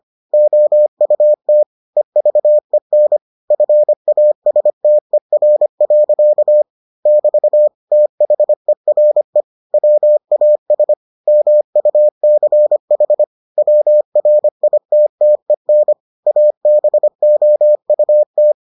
Morse code, heard as a keyed tone, about 25 wpm, 600 Hz. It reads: OUT EVEN FASTER. <BT> THERE WAS MUCH WRITTEN ABOUT